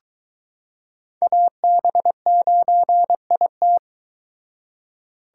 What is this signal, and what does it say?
Morse code, heard as a keyed tone, about 21 wpm, 700 Hz. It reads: AB9IT